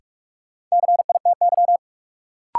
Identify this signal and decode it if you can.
Morse code, heard as a keyed tone, about 46 wpm, 700 Hz. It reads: CITY
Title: Sentences from Top 300 Words - New Word Drill - 45wpm